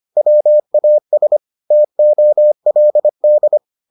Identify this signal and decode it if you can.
Morse code, heard as a keyed tone, about 25 wpm, 600 Hz. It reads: WAS TOLD